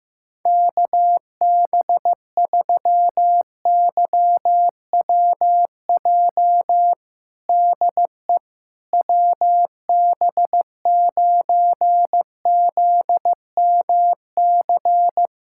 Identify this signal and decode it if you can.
Morse code, heard as a keyed tone, about 14 wpm, 700 Hz. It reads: KB3YWJ DE WB9ZMC
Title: USA Callsign DE Callsign 15wpm